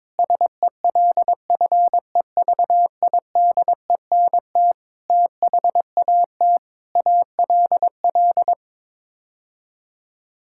Morse code, heard as a keyed tone, about 22 wpm, 700 Hz. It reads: SELFEVIDENT THAT ALL
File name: Sets of 9 Words - 22wpm